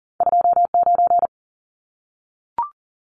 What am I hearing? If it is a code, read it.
Morse code, heard as a keyed tone, about 40 wpm, 700 Hz. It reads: 29
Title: Numbers - 2-Digit 40wpm